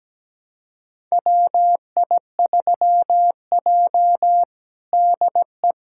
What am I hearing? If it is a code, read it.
Morse code, heard as a keyed tone, about 17 wpm, 700 Hz. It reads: WI3J DE
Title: USA Callsign DE Callsign - 4 Characters - 17wpm